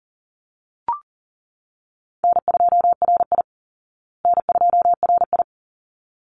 Morse code, heard as a keyed tone, about 40 wpm, 700 Hz. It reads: N2RI N2RI
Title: Easy Call Signs Repeated 2x - 40wpm